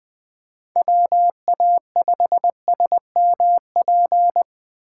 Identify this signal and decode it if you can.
Morse code, heard as a keyed tone, about 20 wpm, 700 Hz. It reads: WA5SMP
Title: USA Call Signs - 6 Characters - 4x Character Spacing - 20wpm